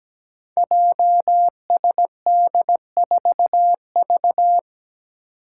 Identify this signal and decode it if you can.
Morse code, heard as a keyed tone, about 17 wpm, 700 Hz. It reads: JSD4V
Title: Five Challenging Letter-Number Combinations 17wpm